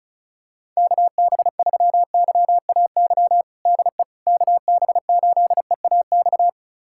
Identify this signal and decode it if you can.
Morse code, heard as a keyed tone, about 34 wpm, 700 Hz. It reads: KB3YAY DE KB8EAX
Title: USA Callsign DE Callsign - 6 Characters - 35wpm